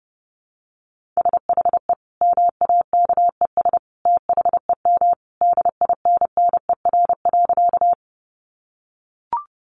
Morse code, heard as a keyed tone, about 30 wpm, 700 Hz. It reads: SHE MAKES THEM DINNER.
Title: Easy Phrases 30wpm